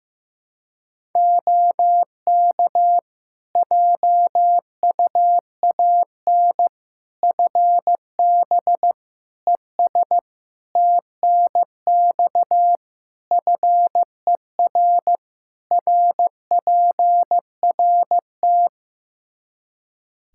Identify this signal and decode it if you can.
Morse code, heard as a keyed tone, about 16 wpm, 700 Hz. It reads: OK JUAN FB ES TNX FER RPRT
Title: K1USN SST QSO Practice 15wpm